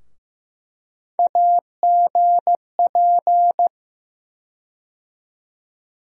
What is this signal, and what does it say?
Morse code, heard as a keyed tone, about 15 wpm, 700 Hz. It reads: AGP
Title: USA Call Sign Postfixes - 3 Letters 15wpm